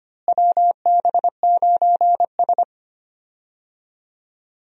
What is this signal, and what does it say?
Morse code, heard as a keyed tone, about 25 wpm, 700 Hz. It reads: WB9S
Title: USA Call Signs - 4 Characters - Repeated 2x - 25wpm